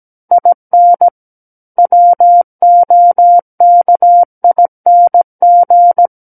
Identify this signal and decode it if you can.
Morse code, heard as a keyed tone, about 17 wpm, 700 Hz. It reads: IN WOKING
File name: Book One / 08-Chapter 8 - 17 WPM